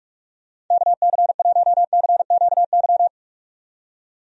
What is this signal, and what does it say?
Morse code, heard as a keyed tone, about 46 wpm, 700 Hz. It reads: KC1CQY
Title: USA Call Signs - 6 Characters - 45wpm